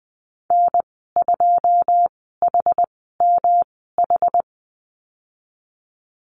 Morse code, heard as a keyed tone, about 20 wpm, spaced as for 14 wpm, 700 Hz. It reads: N2HMH
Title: 154 Lesson 25g - Letter p - Call Signs 20wpm